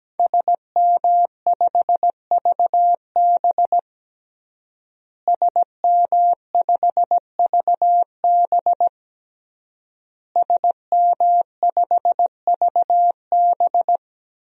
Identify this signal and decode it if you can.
Morse code, heard as a keyed tone, about 17 wpm, 700 Hz. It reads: SM5VB SM5VB SM5VB
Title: DXCC-100 Call Signs - 5 Characters - Repeated 3x - 17wpm